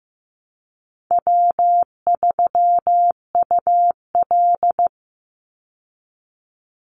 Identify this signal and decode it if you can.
Morse code, heard as a keyed tone, about 15 wpm, 700 Hz. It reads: W3UL
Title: Easy Call Signs Repeated 2x - 15wpm